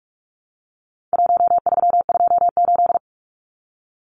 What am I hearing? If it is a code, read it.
Morse code, heard as a keyed tone, about 44 wpm, 700 Hz. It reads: 1328
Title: Numbers - Contest Style - 4-Digit 45wpm